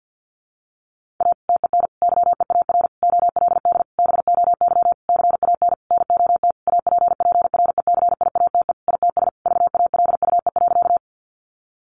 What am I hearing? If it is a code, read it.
Morse code, heard as a keyed tone, about 50 wpm, 700 Hz. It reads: A TEN YEAR OLD BOY CAN NOT APPRECIATE ITS VALUE.
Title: Sentences from Top 1300 Words 50wpm